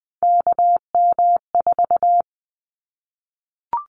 Morse code, heard as a keyed tone, about 20 wpm, 700 Hz. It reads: KM4
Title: Callsign Prefixes for Top 25 DXCC - 20wpm